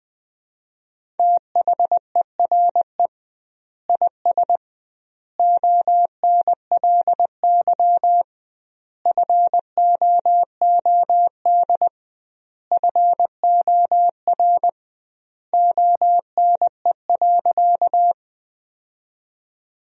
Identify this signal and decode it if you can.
Morse code, heard as a keyed tone, about 20 wpm, 700 Hz. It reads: THERE IS ONLY FOOD FOR ONE.
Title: 2x Word-Spacing - Sentences from Top 300 Words - 20wpm